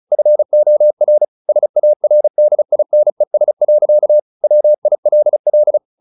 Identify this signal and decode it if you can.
Morse code, heard as a keyed tone, about 34 wpm, 600 Hz. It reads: FOR SARDINES. WILL
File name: the old man and the sea / the old man and the sea 1 35wpm